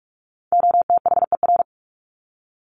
Morse code, heard as a keyed tone, about 46 wpm, 700 Hz. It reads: OTHER
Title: Top 200 Words 45wpm